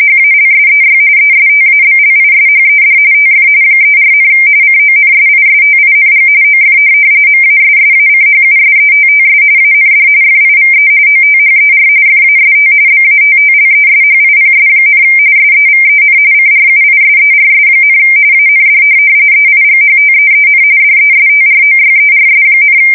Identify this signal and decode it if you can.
RTTY, about 45.45 baud. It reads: HALSEY ESTABLISHED TF 34, CONSISTING OF MASSACHUSETTS AND FIVE OTHER FAST BATTLESHIPS, SEVEN CRUISERS, AND EIGHTEEN DESTROYERS,